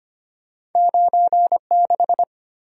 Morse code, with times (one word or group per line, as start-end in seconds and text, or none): 0.8-1.6 9
1.7-2.2 6